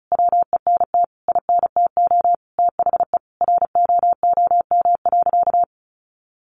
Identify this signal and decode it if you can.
Morse code, heard as a keyed tone, about 34 wpm, 700 Hz. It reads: WENT INTO THE ROOM.
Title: Sentences from Top 600 Words 35wpm